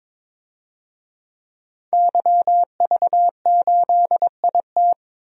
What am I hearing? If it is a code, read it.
Morse code, heard as a keyed tone, about 21 wpm, 700 Hz. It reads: YV8IT
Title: DXCC-100 Call Signs - 5 Characters - Repeated 3x - 22wpm